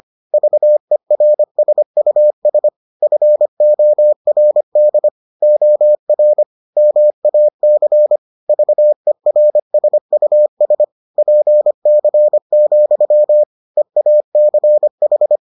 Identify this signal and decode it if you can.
Morse code, heard as a keyed tone, about 25 wpm, 600 Hz. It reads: VERSUS FORD OR MAC VERSUS PC, EACH